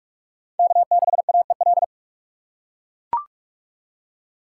Morse code, heard as a keyed tone, about 43 wpm, 700 Hz. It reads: K6AEL
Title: USA Call Signs - 4x Character Spacing - 45wpm